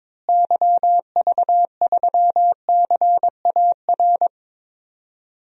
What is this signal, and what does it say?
Morse code, heard as a keyed tone, about 22 wpm, 700 Hz. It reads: YV3CAR
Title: DXCC-100 Call Signs - 6 Characters - Repeated 2x - 22wpm